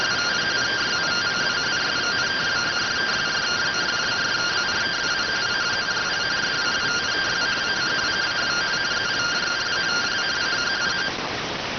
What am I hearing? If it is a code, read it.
Two-tone teleprinter signal, about 170 Hz shift.